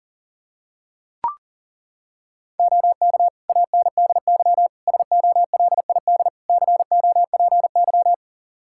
Morse code, heard as a keyed tone, about 37 wpm, 700 Hz. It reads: OK ANDY SOLID COPY
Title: K1USN SST QSO Practice 40wpm